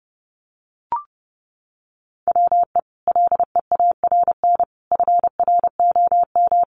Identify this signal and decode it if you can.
Morse code, heard as a keyed tone, about 30 wpm, 700 Hz. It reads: WE LEARN FROM
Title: Sentences from Top 900 Words 30wpm